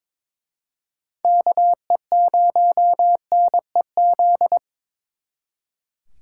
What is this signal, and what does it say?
Morse code, heard as a keyed tone, about 26 wpm, 700 Hz. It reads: KE0NEZ
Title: Speed Racing - USA 2x3 Call Signs 22wpm